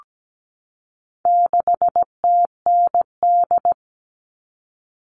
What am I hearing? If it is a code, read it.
Morse code, heard as a keyed tone, about 17 wpm, 700 Hz. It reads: 6TND
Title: All Cut Numbers - 4-Digit 17wpm